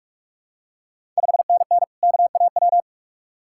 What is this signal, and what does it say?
Morse code, heard as a keyed tone, about 46 wpm, 700 Hz. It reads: 5NN KAW
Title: Ontario QSO Party Exchange 45wpm